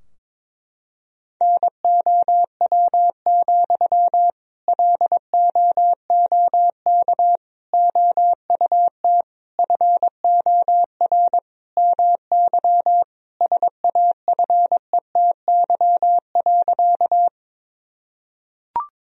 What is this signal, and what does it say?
Morse code, heard as a keyed tone, about 22 wpm, 700 Hz. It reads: NOW, LOOK OUT FOR MY SAFETY.
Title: Sentences from Top 1000 Words - Review Part 4 - 22wpm